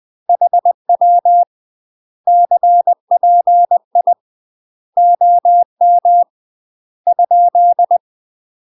Morse code, heard as a keyed tone, about 20 wpm, 700 Hz. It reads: HW CPI OM ?
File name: UK MASTER QSOs TRAINING / 20WPM / UK MASTER QSOs 0070